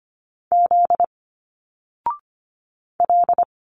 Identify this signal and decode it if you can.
Morse code, heard as a keyed tone, about 25 wpm, 700 Hz. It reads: Z L
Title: Single Letters 25wpm